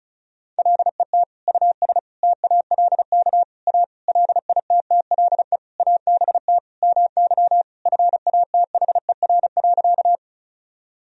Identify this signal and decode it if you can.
Morse code, heard as a keyed tone, about 36 wpm, 700 Hz. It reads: LET US TALK A LITTLE ABT MY FATHER.
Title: Sentences from Top 400 Words and Abbreviations 35wpm